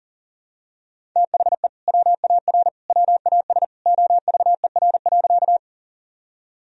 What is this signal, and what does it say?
Morse code, heard as a keyed tone, about 37 wpm, 700 Hz. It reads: THE WAR WAS OVER.
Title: Sentences from Top 1000 Words - Review Part 2 - 40wpm